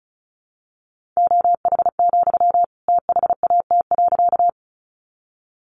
Morse code, heard as a keyed tone, about 34 wpm, 700 Hz. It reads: OH, THAT.
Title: Sentences from Top 500 Words 35wpm